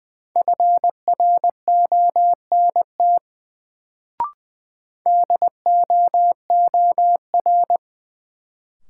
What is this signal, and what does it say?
Morse code, heard as a keyed tone, about 20 wpm, 700 Hz. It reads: FRONT DOOR
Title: Sets of 2 Words - New Word Drill - 20wpm